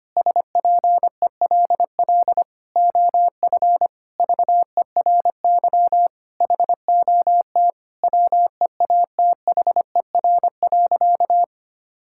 Morse code, heard as a keyed tone, about 25 wpm, 700 Hz. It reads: SPELL OF VERY HOT WEATHER.
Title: Sentences from Top 1000 Words - Review Part 5 - 25wpm